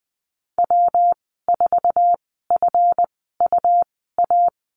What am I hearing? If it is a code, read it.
Morse code, heard as a keyed tone, about 20 wpm, spaced as for 15 wpm, 700 Hz. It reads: W4FUA
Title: 147 Lesson 24g - Letter y - Call Signs 20wpm